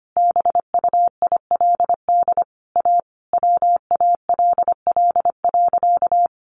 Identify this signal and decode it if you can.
Morse code, heard as a keyed tone, about 25 wpm, 700 Hz. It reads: BUILD A WALL.
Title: Sentences from Top 1100 Words 25wpm